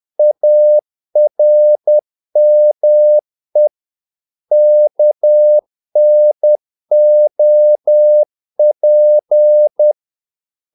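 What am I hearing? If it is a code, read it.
Morse code, heard as a keyed tone, about 10 wpm, 600 Hz. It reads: ARME KNOP
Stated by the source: Morse practice of short Danish words at 10 wpm, from CW med Gnister.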